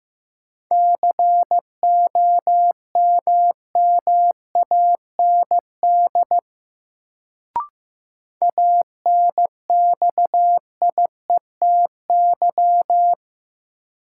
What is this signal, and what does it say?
Morse code, heard as a keyed tone, about 15 wpm, 700 Hz. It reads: COMMAND ANXIETY
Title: Sentences from Top 1600 Words - New Word Drill - 15wpm